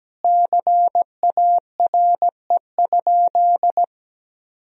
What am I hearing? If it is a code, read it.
Morse code, heard as a keyed tone, about 17 wpm, 700 Hz. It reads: CARE?
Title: Sentences from Top 1000 Words - Review Part 1 - 17wpm